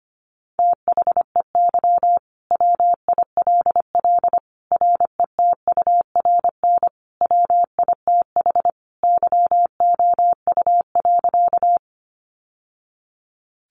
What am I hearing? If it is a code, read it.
Morse code, heard as a keyed tone, about 25 wpm, 700 Hz. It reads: THEY WILL RETURN WITH YOU.
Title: Sentences from Top 800 Words 25wpm